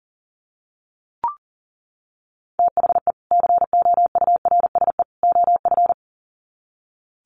Morse code, heard as a keyed tone, about 40 wpm, 700 Hz. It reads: THE COURSE OF
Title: Sets of 3 Words 40wpm